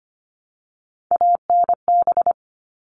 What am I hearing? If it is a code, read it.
Morse code, heard as a keyed tone, about 25 wpm, 700 Hz. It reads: ANB
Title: All Cut Numbers - 3-Digit 25wpm